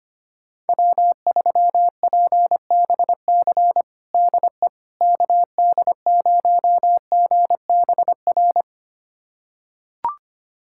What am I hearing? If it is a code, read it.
Morse code, heard as a keyed tone, about 25 wpm, 700 Hz. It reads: W3PBC DE KD0GBR
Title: USA Callsign DE Callsign 25wpm